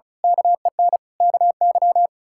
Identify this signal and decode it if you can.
Morse code, heard as a keyed tone, about 36 wpm, 700 Hz. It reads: KEN KY
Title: K1USN SST QSO Practice - 35wpm